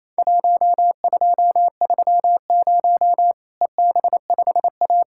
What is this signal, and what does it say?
Morse code, heard as a keyed tone, about 28 wpm, 700 Hz. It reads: 1230 EB5A
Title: Worked All Europe DX Contest - QTC Exchange - 28wpm